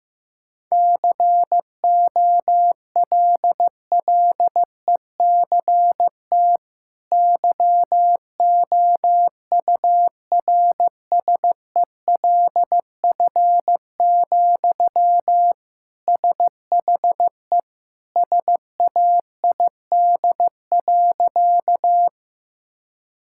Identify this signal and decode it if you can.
Morse code, heard as a keyed tone, about 16 wpm, 700 Hz. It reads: COLLECT YOURSELF, SHE SAID.
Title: Sentences from Top 1600 Words 15wpm